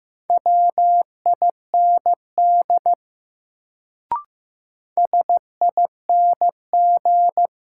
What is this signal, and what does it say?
Morse code, heard as a keyed tone, about 15 wpm, 700 Hz. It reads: WIND SING
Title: Sentences from Top 400 Words - New Word Drill - 15wpm